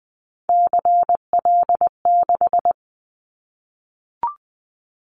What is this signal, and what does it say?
Morse code, heard as a keyed tone, about 20 wpm, 700 Hz. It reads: CL6
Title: Callsign Prefixes for Top 100 DXCC - 20wpm